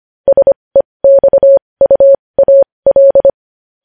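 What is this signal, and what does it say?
Morse code, heard as a keyed tone, about 25 wpm, spaced as for 20 wpm, 550 Hz. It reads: SEXUAL
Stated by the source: Morse code news headlines practice at 20 wpm.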